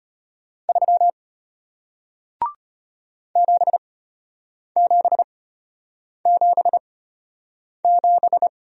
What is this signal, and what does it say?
Morse code, heard as a keyed tone, about 33 wpm, spaced as for 8 wpm, 700 Hz. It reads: 3 7777